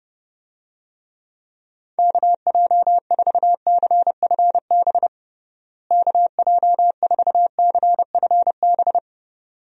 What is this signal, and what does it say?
Morse code, heard as a keyed tone, about 32 wpm, 700 Hz. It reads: KJ4CFB KJ4CFB
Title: USA Call Signs - 6 Characters - Repeated 2x - 30wpm